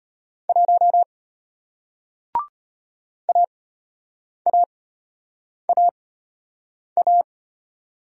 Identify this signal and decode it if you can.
Morse code, heard as a keyed tone, about 33 wpm, spaced as for 8 wpm, 700 Hz. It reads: 1 AAAA